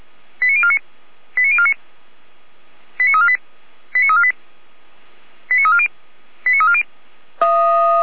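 Five-tone selective call (ZVEI), heard as quick runs of five tones, tones of about 70 ms. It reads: 89049 89049 89348 89348 89340 89340
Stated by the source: ZVEI five-tone selective call received by SDR.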